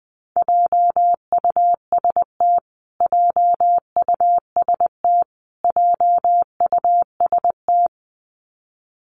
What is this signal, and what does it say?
Morse code, heard as a keyed tone, about 20 wpm, 700 Hz. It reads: JUST JUST JUST
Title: Top 300 Words Repeated 20wpm